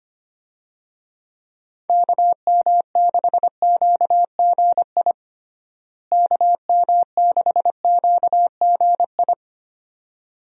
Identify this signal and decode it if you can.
Morse code, heard as a keyed tone, about 25 wpm, 700 Hz. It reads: KM6QGI KM6QGI
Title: USA Call Signs - 6 Characters - Repeated 2x - 25wpm